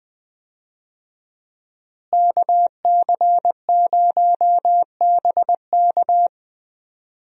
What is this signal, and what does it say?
Morse code, heard as a keyed tone, about 20 wpm, 700 Hz. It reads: KC0BK
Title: USA Call Signs - 5 Characters - Repeated 3x - 20wpm